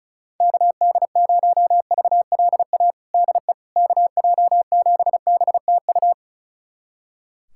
Morse code, heard as a keyed tone, about 36 wpm, 700 Hz. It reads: KD0VLA DE KJ7BTU
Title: USA Callsign DE Callsign - 6 Characters - 35wpm